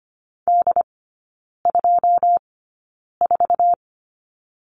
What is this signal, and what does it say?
Morse code, heard as a keyed tone, about 25 wpm, spaced as for 9 wpm, 700 Hz. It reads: D24
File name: course / 25:10wpm / 093 Lesson 16e - Letter c - Three Characters 25wpm